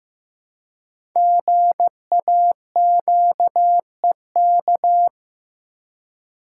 Morse code, heard as a keyed tone, about 15 wpm, 700 Hz. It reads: GAQEK
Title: Code Groups - 5 Letters - 15wpm